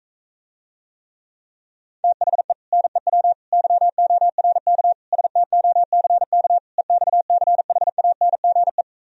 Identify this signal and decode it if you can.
Morse code, heard as a keyed tone, about 45 wpm, 700 Hz. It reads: THE NEW YORK STOCK EXCHANGE